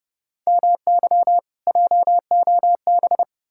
Morse code, heard as a keyed tone, about 37 wpm, 700 Hz. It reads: MY JOB
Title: Speed-Racing Common QSO Phrases Encore - 30wpm